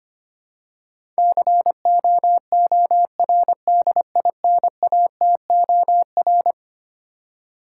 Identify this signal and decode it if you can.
Morse code, heard as a keyed tone, about 25 wpm, 700 Hz. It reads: COORDINATOR
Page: morse-code-ninja-practice-25wpm